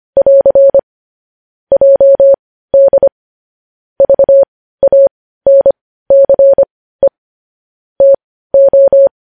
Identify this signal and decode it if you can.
Morse code, heard as a keyed tone, about 25 wpm, spaced as for 15 wpm, 550 Hz. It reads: <AR> JD VANCE TO
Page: news-headlines-in-morse-code-at-15-wpm